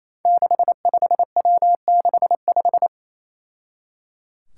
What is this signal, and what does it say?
Morse code, heard as a keyed tone, about 27 wpm, 700 Hz. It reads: 65W65
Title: Five Challenging Letter-Number Combinations 28wpm